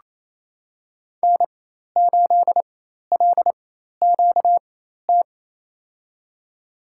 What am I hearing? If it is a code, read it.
Morse code, heard as a keyed tone, about 28 wpm, spaced as for 13 wpm, 700 Hz. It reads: N8LQT
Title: USA Call Signs - 5 Characters - 4x Character Spacing - 28wpm